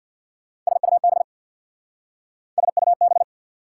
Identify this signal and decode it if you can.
Morse code, heard as a keyed tone, about 62 wpm, 700 Hz. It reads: SUB SUB
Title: Speed-Racing Prefixes and Suffixes 50wpm